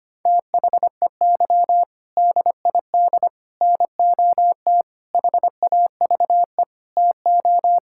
Morse code, heard as a keyed tone, about 25 wpm, 700 Hz. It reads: THEY DID NOT HAVE TO
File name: practice / 25wpm / Sentences with 4-Word Phrases - 25wpm